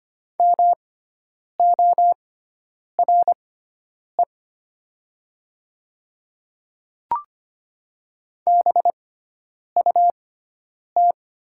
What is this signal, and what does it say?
Morse code, heard as a keyed tone, about 25 wpm, spaced as for 9 wpm, 700 Hz. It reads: MORE BUT
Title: 6x Character Spacing - Top 200 Words - 25wpm